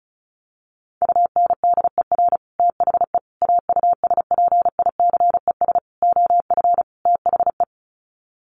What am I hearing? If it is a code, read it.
Morse code, heard as a keyed tone, about 34 wpm, 700 Hz. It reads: UNDER THE AUSPICES OF THE
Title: Sets of 5 Words Encore2 35wpm